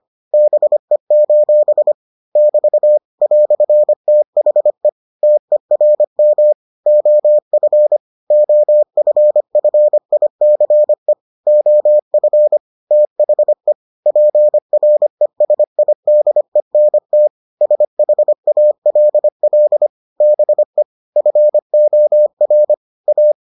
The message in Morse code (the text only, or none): BE: <BT> "THE TERM OF OFFICE OF THE PRESIDENT SHALL BE FOR A